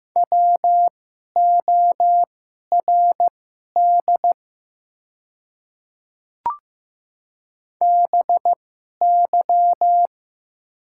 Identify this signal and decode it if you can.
Morse code, heard as a keyed tone, about 15 wpm, spaced as for 11 wpm, 700 Hz. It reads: WORD BY
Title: 2x Character Spacing - Top 100 Words - 15wpm